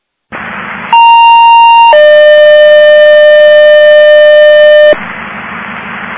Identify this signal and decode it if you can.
Two-tone page, heard as a one-second tone, then a three-second tone, about 930 then 600 Hz, with noise. It is heard from 0.9 to 4.9 s.